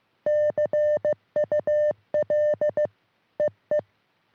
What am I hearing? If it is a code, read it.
Morse code, heard as a keyed tone, about 15 wpm, 590 Hz. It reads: CUL EE
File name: sm7lcb / g4bao all 23cm